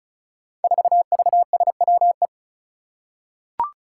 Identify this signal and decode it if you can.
Morse code, heard as a keyed tone, about 34 wpm, 700 Hz. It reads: 4VSWE
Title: Five Challenging Letter-Number Combinations 35wpm